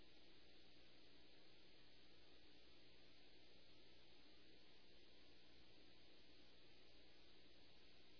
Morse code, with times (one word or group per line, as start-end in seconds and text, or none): none